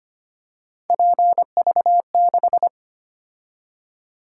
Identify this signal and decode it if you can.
Morse code, heard as a keyed tone, about 25 wpm, 700 Hz. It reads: PV6
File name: Three Letter-Number Combinations - 25wpm